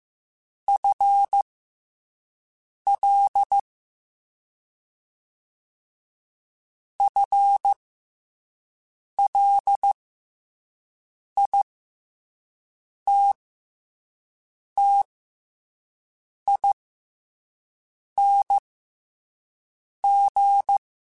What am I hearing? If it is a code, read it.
Morse code, heard as a keyed tone, about 15 wpm, spaced as for 5 wpm, 800 Hz. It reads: FL FLITTING
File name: FL / FL mp3s / FL Prefixes 1x1 @5